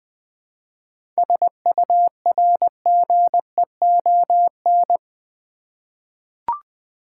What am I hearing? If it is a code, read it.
Morse code, heard as a keyed tone, about 21 wpm, 700 Hz. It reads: SURGEON
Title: 7 Letter Words - First 3 Letters Given - 20wpm